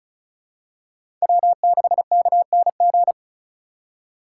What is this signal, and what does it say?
Morse code, heard as a keyed tone, about 34 wpm, 700 Hz. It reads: W6KNG
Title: USA Call Signs - 5 Characters - 3x Character Spacing - 35wpm